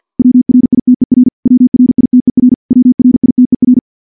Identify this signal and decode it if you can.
Morse code, heard as a keyed tone, about 50 wpm, 250 Hz. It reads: WRITER WRITER WRITER
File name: morse / morse-training-50WPM